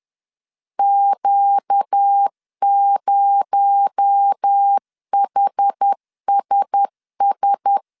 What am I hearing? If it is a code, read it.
Morse code, heard as a keyed tone, about 10 wpm, 800 Hz. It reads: Q0HSS